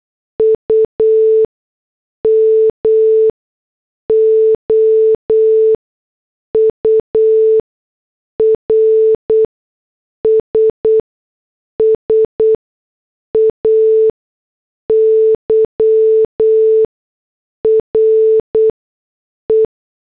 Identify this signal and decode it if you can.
Morse code, heard as a keyed tone, about 8 wpm, spaced as for 6 wpm, 440 Hz. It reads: UMOURS SAY RE
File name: morse / headlines